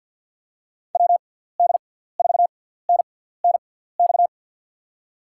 Morse code, heard as a keyed tone, about 52 wpm, spaced as for 17 wpm, 700 Hz. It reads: WD4NNX
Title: USA Call Signs - 6 Characters - 6x Character Spacing - 50wpm